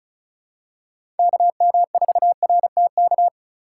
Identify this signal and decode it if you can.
Morse code, heard as a keyed tone, about 36 wpm, 690 Hz. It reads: KM4RTK
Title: USA Call Signs - 6 Characters - 2x Character Spacing - 35wpm